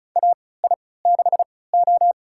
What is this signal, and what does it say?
Morse code, heard as a keyed tone, about 34 wpm, spaced as for 20 wpm, 700 Hz. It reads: AI6O
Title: USA Call Signs - 4 Characters - 3x Character Spacing - 35wpm